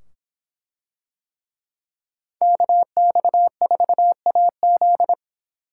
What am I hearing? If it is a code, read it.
Morse code, heard as a keyed tone, about 25 wpm, 700 Hz. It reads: KX4AZ